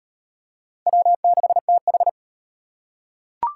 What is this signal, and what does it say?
Morse code, heard as a keyed tone, about 34 wpm, 700 Hz. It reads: W6TH